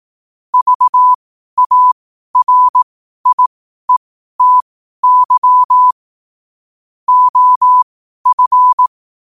Morse code, heard as a keyed tone, about 18 wpm, spaced as for 12 wpm, 1000 Hz. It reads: VARIETY OF
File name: qotdfiles / 12wpm